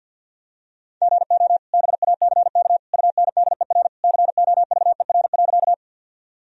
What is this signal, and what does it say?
Morse code, heard as a keyed tone, about 52 wpm, 700 Hz. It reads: GO BACK UNDER COVER.
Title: Sentences from Top 1000 Words - Review Part 1 - 50wpm